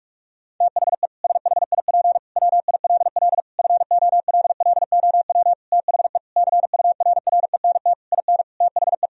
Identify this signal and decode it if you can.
Morse code, heard as a keyed tone, about 46 wpm, 700 Hz. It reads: THE SHIP WILL FOLLOW THE CURRENT IN THE
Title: Sentences from Top 1000 Words - Review Part 4 - 45wpm